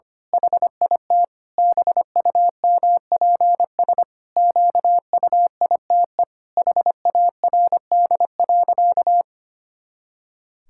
Morse code, heard as a keyed tone, about 25 wpm, 700 Hz. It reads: HIT BUMPS QUITE HARD.
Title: Pangram Challenge v2 - 25wpm